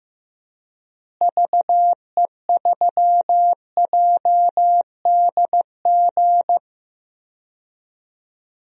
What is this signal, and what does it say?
Morse code, heard as a keyed tone, about 15 wpm, 700 Hz. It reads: VE3JDG
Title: DXCC-100 Call Signs - 6 Characters - Repeated 3x - 15wpm